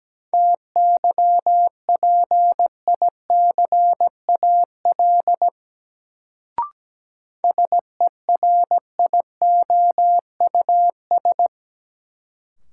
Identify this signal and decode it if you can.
Morse code, heard as a keyed tone, about 17 wpm, 700 Hz. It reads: TYPICAL SERIOUS
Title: Sentences from Top 1100 Words - New Word Drill Alt - 17wpm